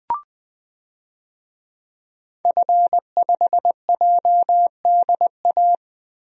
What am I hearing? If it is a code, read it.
Morse code, heard as a keyed tone, about 20 wpm, 700 Hz. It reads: F5JDA
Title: DXCC-100 Call Signs - 5 Characters - Repeated 3x - 20wpm